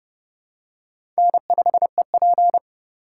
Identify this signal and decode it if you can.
Morse code, heard as a keyed tone, about 30 wpm, 700 Hz. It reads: N5EP
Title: USA Call Signs - 4 Characters - 3x Character Spacing - 30wpm